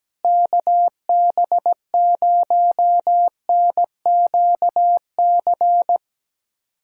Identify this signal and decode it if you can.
Morse code, heard as a keyed tone, about 17 wpm, 700 Hz. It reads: KB0NQC
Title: USA Callsign DE Callsign - 6 Characters - 17wpm